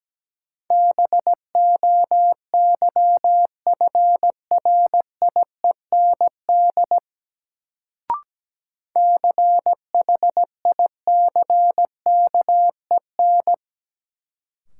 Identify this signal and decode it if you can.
Morse code, heard as a keyed tone, about 17 wpm, 700 Hz. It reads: BOYFRIEND CHICKEN
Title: Sentences from Top 1000 Words - Review Part 5 - New Word Drill - 17wpm